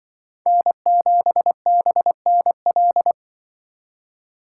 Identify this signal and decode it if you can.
Morse code, heard as a keyed tone, about 23 wpm, 700 Hz. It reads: N7BNL
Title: Speed Racing - USA 1x3 Call Signs 20wpm